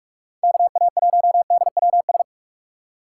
Morse code, heard as a keyed tone, about 44 wpm, 700 Hz. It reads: KA1DWS